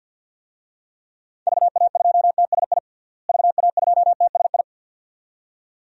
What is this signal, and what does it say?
Morse code, heard as a keyed tone, about 53 wpm, 700 Hz. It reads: VA2TSI VA2TSI
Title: DXCC-100 Call Signs - 6 Characters - Repeated 2x - 50wpm